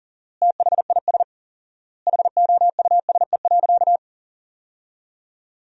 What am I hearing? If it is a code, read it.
Morse code, heard as a keyed tone, about 40 wpm, 700 Hz. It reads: THIS HOUSE.
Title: 4x Word-Spacing - Sentences from Top 400 Words - 40wpm